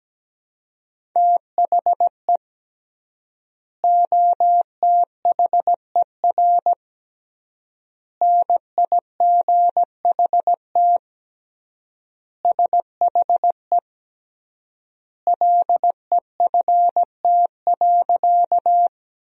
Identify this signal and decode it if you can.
Morse code, heard as a keyed tone, about 17 wpm, 700 Hz. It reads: THE OTHER NIGHT SHE LEFT.
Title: 3x Word-Spacing - Sentences from Top 300 Words - 17wpm